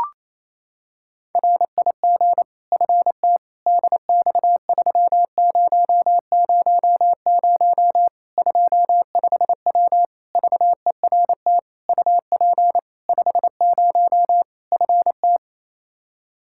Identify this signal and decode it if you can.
Morse code, heard as a keyed tone, about 27 wpm, 700 Hz. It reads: RIG FT DX3000 25W VERT UP 50 FT
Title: Practice QSO - Exchange 2 - Info - 28wpm